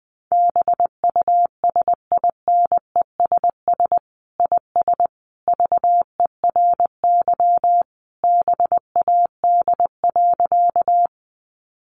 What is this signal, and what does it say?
Morse code, heard as a keyed tone, about 20 wpm, 700 Hz. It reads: BUSINESS IS VERY BAD.
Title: Sentences from Top 700 Words 20wpm